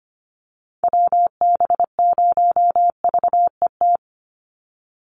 Morse code, heard as a keyed tone, about 25 wpm, 700 Hz. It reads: WB0VET
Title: USA Call Signs 25wpm